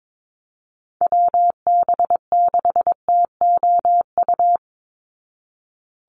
Morse code, heard as a keyed tone, about 22 wpm, 700 Hz. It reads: WB6TOU
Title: Call Signs Repeated 2x - 22wpm